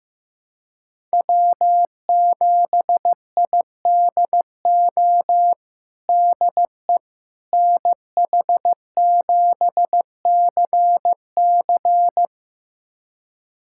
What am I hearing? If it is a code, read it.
Morse code, heard as a keyed tone, about 16 wpm, 700 Hz. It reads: W7IDO DE NH7CC
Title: USA Callsign DE Callsign - 5 Characters - 15wpm